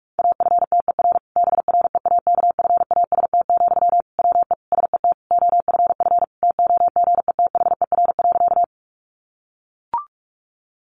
Morse code, heard as a keyed tone, about 44 wpm, 700 Hz. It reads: AFTER BREAKFAST, WE SET OFF TOGETHER.
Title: Sentences from Top 1500 Words 45wpm